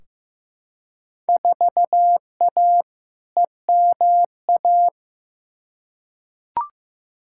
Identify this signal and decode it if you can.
Morse code, heard as a keyed tone, about 15 wpm, 700 Hz. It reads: 4A EMA
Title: Field Day Exchanges - 15wpm